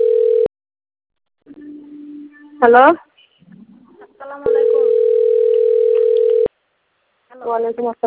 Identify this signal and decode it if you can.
Ringback tone (North America), 440 + 480 Hz, with speech.
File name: autotestagent / 11-31-15.136..callcentertransfer. .ca6881bd-bf5a-4b35-8564